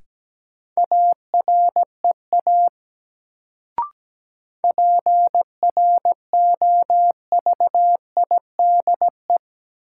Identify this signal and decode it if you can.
Morse code, heard as a keyed tone, about 17 wpm, 700 Hz. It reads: AREA PROVIDE
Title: Sentences from Top 1000 Words - Review Part 2 - New Word Drill - 17wpm